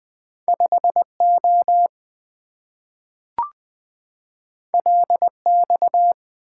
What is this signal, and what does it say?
Morse code, heard as a keyed tone, about 20 wpm, 700 Hz. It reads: 5O LX
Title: Two Letter-Number Combinations - 20wpm